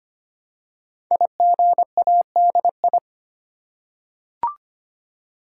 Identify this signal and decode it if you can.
Morse code, heard as a keyed tone, about 25 wpm, 700 Hz. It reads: IGADI
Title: Code Groups - 5 Letters - 25wpm